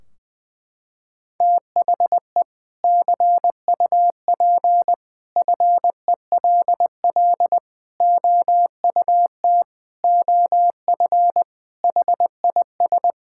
Morse code, heard as a keyed tone, about 20 wpm, 700 Hz. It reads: THE CUP FELL OUT OF HIS
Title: Sentences from Top 1600 Words 20wpm